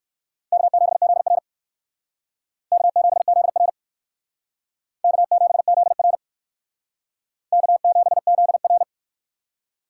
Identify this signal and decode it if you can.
Morse code, heard as a keyed tone, about 58 wpm, 700 Hz. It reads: K7ZR K7ZR K7ZR K7ZR